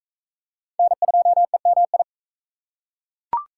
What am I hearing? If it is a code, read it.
Morse code, heard as a keyed tone, about 41 wpm, 700 Hz. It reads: N2EMI